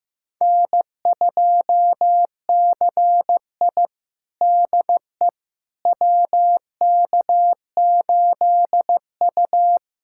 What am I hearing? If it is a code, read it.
Morse code, heard as a keyed tone, about 16 wpm, 700 Hz. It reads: N2CI DE WK8U
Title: USA Callsign DE Callsign - 4 Characters - 15wpm